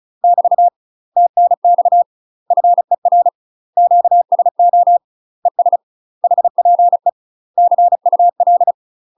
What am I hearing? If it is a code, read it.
Morse code, heard as a keyed tone, about 36 wpm, 700 Hz. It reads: <BT> TNX FER QSO ES HPE CUL
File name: UK MASTER QSOs TRAINING / 35WPM / UK MASTER QSOs 0002